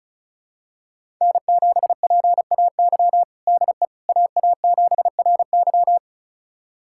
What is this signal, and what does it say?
Morse code, heard as a keyed tone, about 36 wpm, 700 Hz. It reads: N7PAY DE AA7RY
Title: USA Callsign DE Callsign - 5 Characters - 35wpm